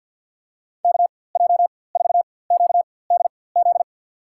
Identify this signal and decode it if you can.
Morse code, heard as a keyed tone, about 53 wpm, spaced as for 24 wpm, 700 Hz. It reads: KJ4QDZ
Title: USA Call Signs - 4x Character Spacing - 50wpm